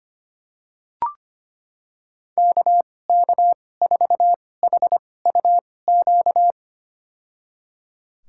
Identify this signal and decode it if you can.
Morse code, heard as a keyed tone, about 25 wpm, spaced as for 18 wpm, 700 Hz. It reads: KK4HUQ
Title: USA Call Signs - 6 Characters - 2x Character Spacing - 25wpm